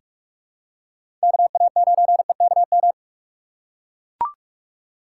Morse code, heard as a keyed tone, about 42 wpm, 700 Hz. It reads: KA9EKM
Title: USA Call Signs - 6 Characters - 3x Character Spacing - 45wpm